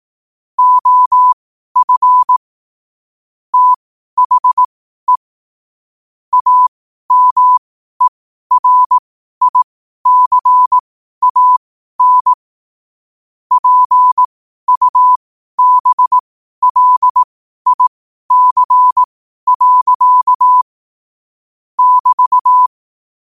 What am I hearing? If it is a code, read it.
Morse code, heard as a keyed tone, about 18 wpm, spaced as for 12 wpm, 1000 Hz. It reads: OF THE AMERICAN PUBLIC. <BT>
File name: qotdfiles / 12wpm